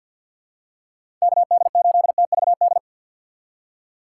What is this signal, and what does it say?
Morse code, heard as a keyed tone, about 50 wpm, 700 Hz. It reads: KD8TVD
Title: USA Call Signs - 6 Characters - 6x Character Spacing - 50wpm